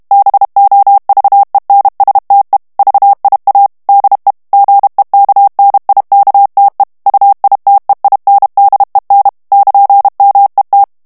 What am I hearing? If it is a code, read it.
Morse code, heard as a keyed tone, about 32 wpm, 800 Hz. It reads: BOVENSTE VIA DE GEKNIKTE UITEINDEN <KN>MET
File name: HOLLANDAIS texte clair 1 vitesse 30 mots minute